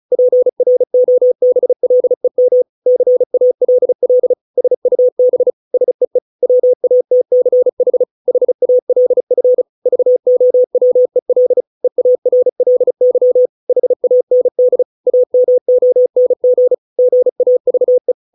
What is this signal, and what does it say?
Morse code, heard as a keyed tone, about 35 wpm, 500 Hz. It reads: PROBLEM CALL SUB SEE WATCH HALF VOWEL EARLY HAND AMONG GAVE